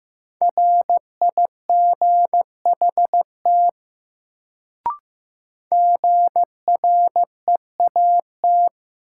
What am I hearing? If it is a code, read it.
Morse code, heard as a keyed tone, about 15 wpm, 700 Hz. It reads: RIGHT GREAT
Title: Sets of 2 Words - New Word Drill - 15wpm